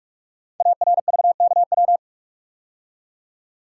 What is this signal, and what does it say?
Morse code, heard as a keyed tone, about 44 wpm, 690 Hz. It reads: ARVKW
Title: Code Groups - 5 Letters - 45wpm